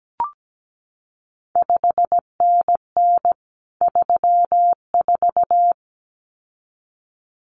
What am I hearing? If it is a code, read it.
Morse code, heard as a keyed tone, about 17 wpm, 700 Hz. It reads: 5NN 34
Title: CQ WW DX Contest Exchange Encore - 17wpm